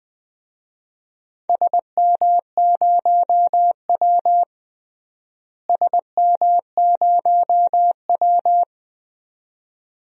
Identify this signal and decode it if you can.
Morse code, heard as a keyed tone, about 20 wpm, 700 Hz. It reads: SM0W SM0W
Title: DXCC-100 Call Signs - 4 Characters - Repeated 2x - 20wpm